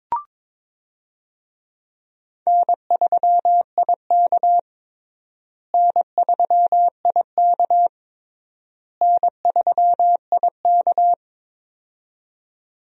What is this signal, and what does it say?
Morse code, heard as a keyed tone, about 22 wpm, 700 Hz. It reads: N3IK N3IK N3IK
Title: USA Call Signs - 4 Characters - Repeated 3x - 22wpm